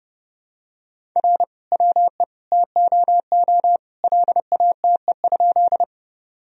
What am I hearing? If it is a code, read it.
Morse code, heard as a keyed tone, about 32 wpm, 700 Hz. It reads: R WE TOO LATE?
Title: Sentences from Top 300 Words and Abbreviations 30wpm